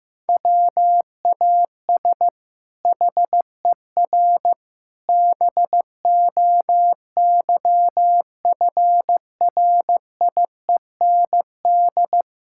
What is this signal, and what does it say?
Morse code, heard as a keyed tone, about 16 wpm, 700 Hz. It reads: WAS HER BOYFRIEND
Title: Sentences from Top 1000 Words - Review Part 5 - 15wpm